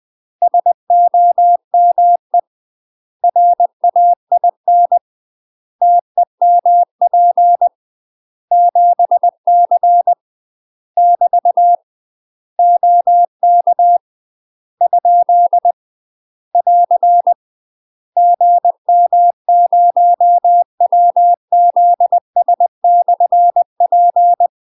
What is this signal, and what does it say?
Morse code, heard as a keyed tone, about 20 wpm, 700 Hz. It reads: SOME RAIN TEMP 7C <BT> OK ? <AR> GM0WZS/P
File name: UK MASTER QSOs TRAINING / 20WPM / UK MASTER QSOs 0058